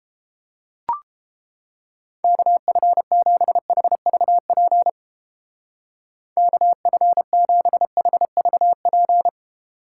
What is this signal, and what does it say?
Morse code, heard as a keyed tone, about 31 wpm, 700 Hz. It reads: KF7HVP KF7HVP